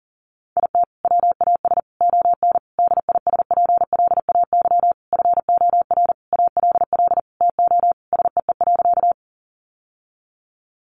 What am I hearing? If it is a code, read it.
Morse code, heard as a keyed tone, about 37 wpm, 700 Hz. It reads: IT WAS ON DISPLAY FOR ALL TO SEE.
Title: Sentences from Top 1200 Words 40wpm